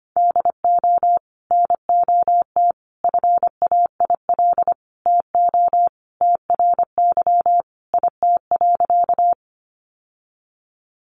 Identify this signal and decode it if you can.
Morse code, heard as a keyed tone, about 25 wpm, 700 Hz. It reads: DO NOT FAIL TO TRY IT.
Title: Sentences from Top 1100 Words 25wpm